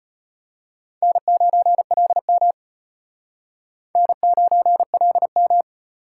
Morse code, heard as a keyed tone, about 31 wpm, 700 Hz. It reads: N9LM N9LM